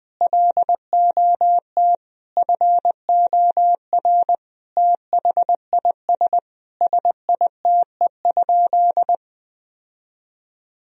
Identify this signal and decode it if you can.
Morse code, heard as a keyed tone, about 20 wpm, 700 Hz. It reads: LOT FOR THIS SITE?
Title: Sentences from Top 1000 Words - Review Part 4 - 20wpm